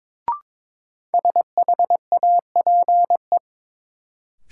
Morse code, heard as a keyed tone, about 22 wpm, 700 Hz. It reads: SHAPE
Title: Sentences from Top 500 Words - New Word Drill - 22wpm